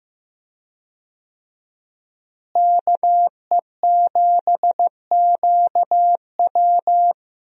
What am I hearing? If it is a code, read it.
Morse code, heard as a keyed tone, about 15 wpm, 700 Hz. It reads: KE7QW
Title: USA Call Signs - 5 Characters - Repeated 2x - 15wpm